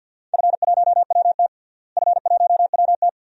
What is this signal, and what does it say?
Morse code, heard as a keyed tone, about 52 wpm, 700 Hz. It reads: F1WT F1WT
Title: DXCC-100 Call Signs - 4 Characters - Repeated 3x - 50wpm